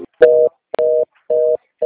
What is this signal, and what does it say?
Reorder tone (North America), 480 + 620 Hz.
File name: out- - - - - .19019